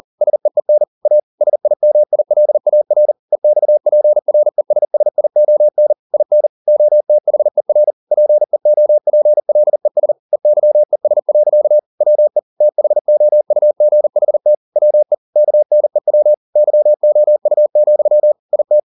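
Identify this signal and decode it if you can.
Morse code, heard as a keyed tone, about 40 wpm, 600 Hz. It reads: SEEN A SIMILAR EXPRESSION IN OTHER PEOPLES EYES. WE THOUGHT WE KNEW YOU, IT
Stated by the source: Morse code at 40 wpm, practice text, CW med Gnister.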